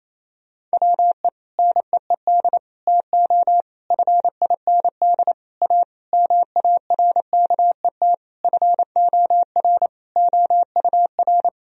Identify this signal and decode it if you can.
Morse code, heard as a keyed tone, about 28 wpm, 700 Hz. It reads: WE NEED TO FIND A MARKET FOR OUR
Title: Sentences from Top 1000 Words - Review Part 3 - 28wpm